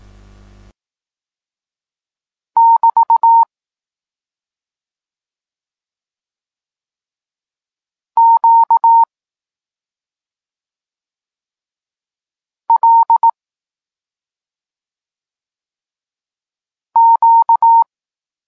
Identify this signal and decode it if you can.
Morse code, heard as a keyed tone, about 18 wpm, 920 Hz. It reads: <BT> Q L Q
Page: Tekst 32 – Doskonalenie odbioru poznanych liter